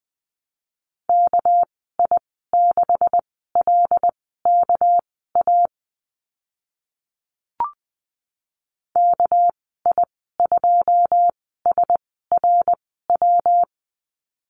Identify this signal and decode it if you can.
Morse code, heard as a keyed tone, about 20 wpm, spaced as for 14 wpm, 700 Hz. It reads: KI6LKA KI2SRW
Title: 200 Lesson 32g - Letter k - Call Signs 20wpm